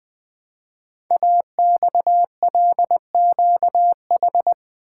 Morse code, heard as a keyed tone, about 20 wpm, 700 Hz. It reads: AXLQH
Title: Code Groups - 5 Letters - 20wpm